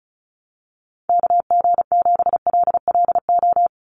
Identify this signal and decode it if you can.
Morse code, heard as a keyed tone, about 34 wpm, 700 Hz. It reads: KG7LLO
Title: USA Call Signs 35wpm